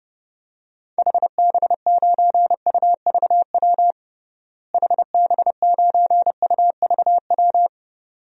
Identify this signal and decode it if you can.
Morse code, heard as a keyed tone, about 30 wpm, 700 Hz. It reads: HB9UVW HB9UVW
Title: DXCC-100 Call Signs - 6 Characters - Repeated 2x - 30wpm